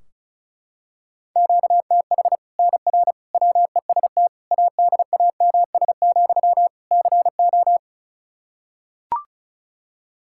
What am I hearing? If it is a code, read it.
Morse code, heard as a keyed tone, about 36 wpm, 700 Hz. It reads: QTH NR WEST ADAMS, CO
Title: Practice QSO - Exchange 1 - Beginning - 35wpm